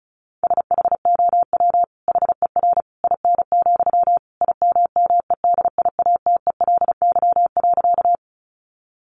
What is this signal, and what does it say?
Morse code, heard as a keyed tone, about 34 wpm, 700 Hz. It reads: SHOW HER IN, IMMEDIATELY.
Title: Sentences from Top 1100 Words 35wpm